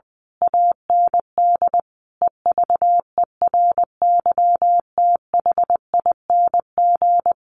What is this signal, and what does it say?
Morse code, heard as a keyed tone, about 20 wpm, 700 Hz. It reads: AND EVERYTHING
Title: A Sound Of Thunder 20wpm